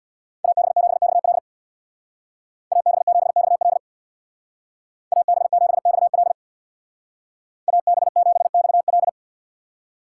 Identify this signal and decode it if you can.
Morse code, heard as a keyed tone, about 59 wpm, 700 Hz. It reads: AB7XL AB7XL AB7XL AB7XL
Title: Speed Racing - USA 2x2 Call Signs 50wpm